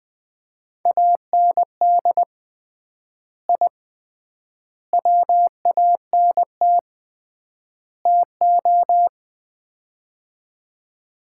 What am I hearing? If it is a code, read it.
Morse code, heard as a keyed tone, about 20 wpm, 700 Hz. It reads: AND I WANT TO
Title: 3x Word-Spacing - Sets of 4 Words - 20wpm